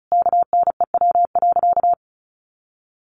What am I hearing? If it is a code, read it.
Morse code, heard as a keyed tone, about 34 wpm, 700 Hz. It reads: KNEW.
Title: Sentences from Top 400 Words 35wpm